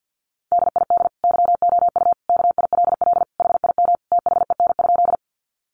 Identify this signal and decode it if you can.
Morse code, heard as a keyed tone, about 50 wpm, 700 Hz. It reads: DID YOU KILL HIM THEN?
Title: Sentences from Top 1400 Words 50wpm